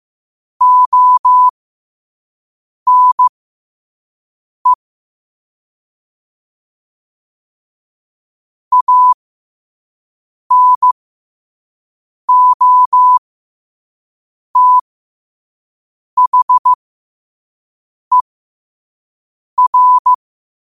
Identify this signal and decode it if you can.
Morse code, heard as a keyed tone, about 15 wpm, spaced as for 5 wpm, 1000 Hz. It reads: ONE ANOTHER
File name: qotdfiles / 5wpm